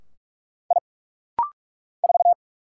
Morse code, heard as a keyed tone, about 46 wpm, 700 Hz. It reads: I 4
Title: Single Letter-Number - Rapid-Fire 45wpm